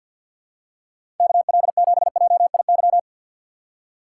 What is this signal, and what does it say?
Morse code, heard as a keyed tone, about 52 wpm, 700 Hz. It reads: KL7JIY
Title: USA Call Signs - 6 Characters - 3x Character Spacing - 50wpm